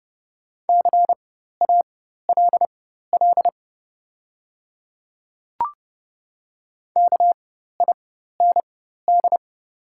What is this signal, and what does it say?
Morse code, heard as a keyed tone, about 30 wpm, spaced as for 14 wpm, 700 Hz. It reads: CALL KIND
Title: 4x Character Spacing - Top 500 Words - 30wpm